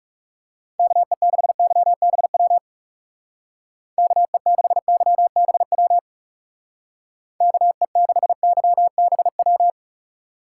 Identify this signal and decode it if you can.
Morse code, heard as a keyed tone, about 36 wpm, 700 Hz. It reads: KE6YBW KE6YBW KE6YBW
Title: Speed Racing - USA 2x3 Call Signs 30wpm